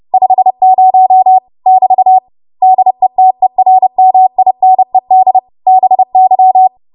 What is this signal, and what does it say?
Morse code, heard as a keyed tone, about 30 wpm, 750 Hz. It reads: 50 <BT> DETERMINED BY
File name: Morse / 30WPM